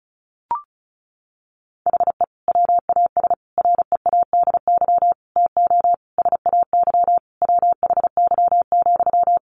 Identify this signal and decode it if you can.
Morse code, heard as a keyed tone, about 34 wpm, 700 Hz. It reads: HE WAS READY TO SAY WHY,
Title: Sentences from Top 1300 Words 35wpm